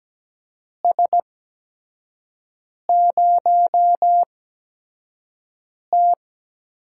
Morse code, heard as a keyed tone, about 17 wpm, spaced as for 5 wpm, 700 Hz. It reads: S0T
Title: USA Call Signs - 4 Characters - 8x Character Spacing - 17wpm